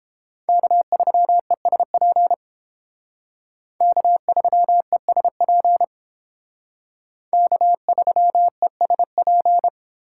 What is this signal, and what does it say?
Morse code, heard as a keyed tone, about 30 wpm, 700 Hz. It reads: K3ESP K3ESP K3ESP